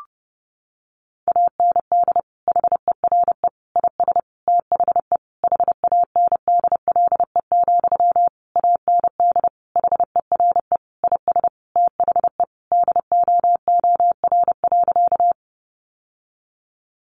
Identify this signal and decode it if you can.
Morse code, heard as a keyed tone, about 30 wpm, 700 Hz. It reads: AND HERE IS THE HANDLE, AND HERE IS THE DOOR.
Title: Sentences from Top 1100 Words 30wpm